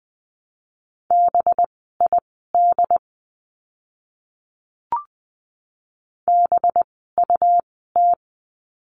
Morse code, heard as a keyed tone, about 20 wpm, spaced as for 14 wpm, 700 Hz. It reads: BID BUT
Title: 185 Lesson 30f - Letter b - Words 20wpm